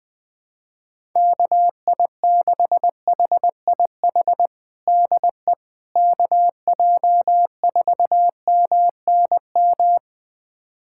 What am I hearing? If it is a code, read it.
Morse code, heard as a keyed tone, about 21 wpm, 700 Hz. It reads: KI6HIH DE KJ4MNM
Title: USA Callsign DE Callsign 20wpm